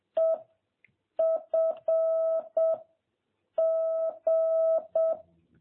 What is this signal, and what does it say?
Morse code, heard as a keyed tone, about 7 wpm, 660 Hz. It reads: E F G
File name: jh8cbh / 2021cw